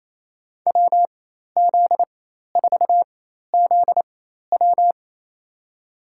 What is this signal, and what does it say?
Morse code, heard as a keyed tone, about 28 wpm, spaced as for 13 wpm, 700 Hz. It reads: WZ4ZW
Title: USA Call Signs - 4x Character Spacing - 28wpm